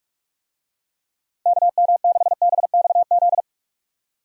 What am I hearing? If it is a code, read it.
Morse code, heard as a keyed tone, about 44 wpm, 700 Hz. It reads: KM6BXZ
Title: USA Call Signs - 6 Characters - 45wpm